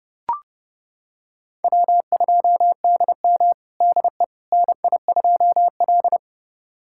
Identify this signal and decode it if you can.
Morse code, heard as a keyed tone, about 29 wpm, 700 Hz. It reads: W2DM DE NI2L
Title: USA Callsign DE Callsign - 4 Characters - 30wpm